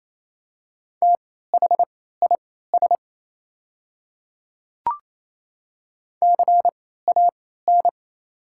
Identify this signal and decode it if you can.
Morse code, heard as a keyed tone, about 28 wpm, spaced as for 16 wpm, 690 Hz. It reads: THIS CAN
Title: 3x Character Spacing - Top 200 Words - 28wpm